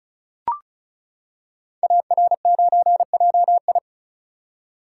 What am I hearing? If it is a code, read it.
Morse code, heard as a keyed tone, about 36 wpm, 690 Hz. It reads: AR9JI
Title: Code Groups - 5 Letter:Numbers - 35wpm